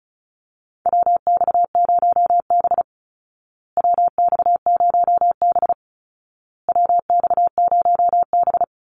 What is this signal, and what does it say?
Morse code, heard as a keyed tone, about 34 wpm, 700 Hz. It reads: WX0B WX0B WX0B
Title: Call Signs Repeated 2x - 35wpm